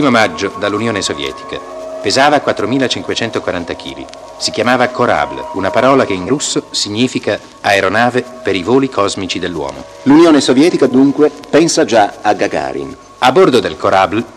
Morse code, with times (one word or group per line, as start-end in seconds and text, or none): none